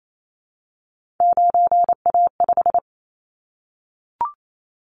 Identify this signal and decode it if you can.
Morse code, heard as a keyed tone, about 28 wpm, 700 Hz. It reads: 9A5
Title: Callsign Prefixes for Top 50 DXCC - 28wpm